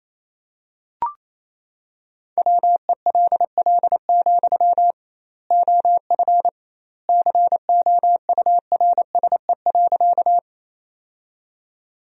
Morse code, heard as a keyed tone, about 28 wpm, 700 Hz. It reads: WELL, OF COURSE.
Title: 2x Word-Spacing - Sentences from Top 500 Words - 28wpm